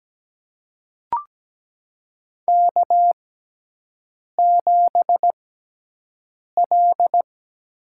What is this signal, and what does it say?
Morse code, heard as a keyed tone, about 17 wpm, spaced as for 6 wpm, 700 Hz. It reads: K7L
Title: USA Call Signs - 4 Characters - 6x Character Spacing - 17wpm